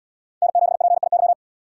Morse code, heard as a keyed tone, about 70 wpm, 700 Hz. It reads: N7PIY